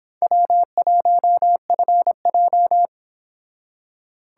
Morse code, heard as a keyed tone, about 25 wpm, 700 Hz. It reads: W1FJ